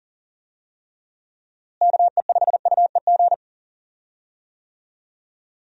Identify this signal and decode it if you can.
Morse code, heard as a keyed tone, about 37 wpm, 700 Hz. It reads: KE5UEG
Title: USA Call Signs - 6 Characters - Repeated 2x - 40wpm